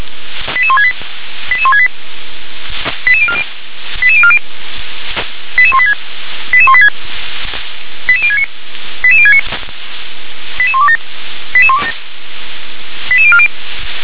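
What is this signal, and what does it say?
Five-tone selective call (ZVEI), heard as quick runs of five tones, tones of about 70 ms. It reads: 88168 88168 88040 88040 88186 88186 88069 88069 88127 88127 88040
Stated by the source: ZVEI five-tone selective call received on an SDR.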